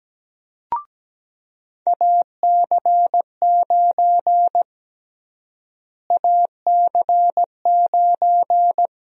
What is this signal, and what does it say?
Morse code, heard as a keyed tone, about 17 wpm, 700 Hz. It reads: AC9 AC9
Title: USA Call Sign Prefixes - Repeated 2x - 17wpm